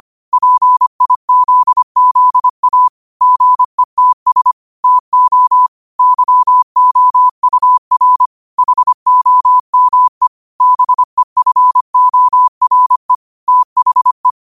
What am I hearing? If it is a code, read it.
Morse code, heard as a keyed tone, about 25 wpm, 1000 Hz. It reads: PIZZA GETS TO YOUR HOME BEFORE THE
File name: qotdfiles / 25wpm